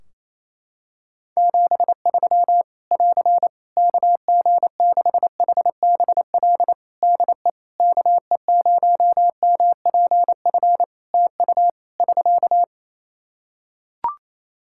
Morse code, heard as a keyed tone, about 28 wpm, 700 Hz. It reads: 73 <AR> KG6HBL DE KE0MPF TU <SK>
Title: Practice QSO - Exchange 4 - Ending - 28wpm